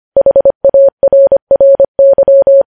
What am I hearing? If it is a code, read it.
Morse code, heard as a keyed tone, about 25 wpm, 550 Hz. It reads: HARRY
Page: news-headlines-in-morse-code-at-25-wpm